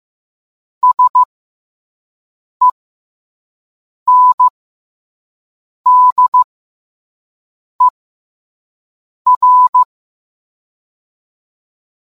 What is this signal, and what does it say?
Morse code, heard as a keyed tone, about 15 wpm, spaced as for 5 wpm, 1000 Hz. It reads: SENDER